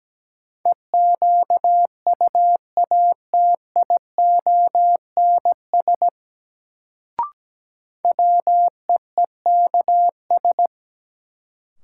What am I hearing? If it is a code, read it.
Morse code, heard as a keyed tone, about 17 wpm, 700 Hz. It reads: EQUATIONS WEEKS
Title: Sets of 5 Words - New Word Drill - 17wpm